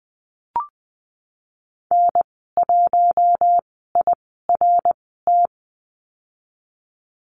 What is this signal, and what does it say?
Morse code, heard as a keyed tone, about 20 wpm, spaced as for 14 wpm, 700 Hz. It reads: N1IRT
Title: 055 Lesson 10g - Letter r - Call Signs 20wpm